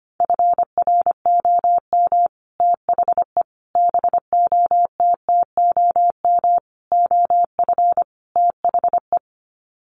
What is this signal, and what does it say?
Morse code, heard as a keyed tone, about 25 wpm, 700 Hz. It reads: FROM THE BOTTOM OF THE
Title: Sets of 5 Words Encore2 25wpm